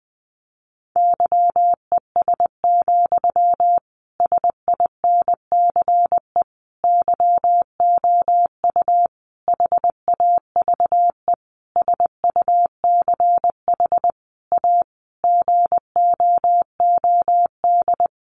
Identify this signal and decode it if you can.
Morse code, heard as a keyed tone, about 20 wpm, 700 Hz. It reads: YES, SINCE YOU HAVE SUCH A GOOD
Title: Sentences from Top 1300 Words 20wpm